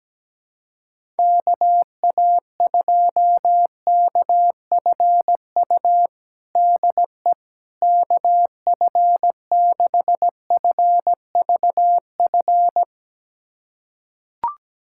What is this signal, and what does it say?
Morse code, heard as a keyed tone, about 17 wpm, 700 Hz. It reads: KA2KFU DE KF6FVF
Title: USA Callsign DE Callsign - 6 Characters - 17wpm